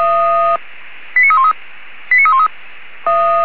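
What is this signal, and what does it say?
Five-tone selective call (ZVEI), heard as quick runs of five tones, tones of about 70 ms. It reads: 89313 89313
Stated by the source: ZVEI received on SDR, from the audio server of the Feuerwehr Orlamünde (fire brigade).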